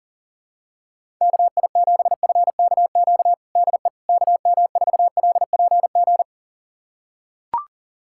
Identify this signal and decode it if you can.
Morse code, heard as a keyed tone, about 37 wpm, 700 Hz. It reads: KI7FKQ DE KM4LPG
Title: USA Callsign DE Callsign - 6 Characters - 40wpm